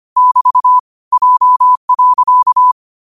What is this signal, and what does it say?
Morse code, heard as a keyed tone, about 25 wpm, 1000 Hz. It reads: <BT> J.